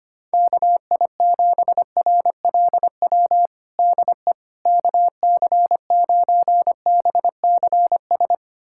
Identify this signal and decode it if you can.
Morse code, heard as a keyed tone, about 25 wpm, 700 Hz. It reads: KI7RLW DE KC9BCS
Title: USA Callsign DE Callsign - 6 Characters - 25wpm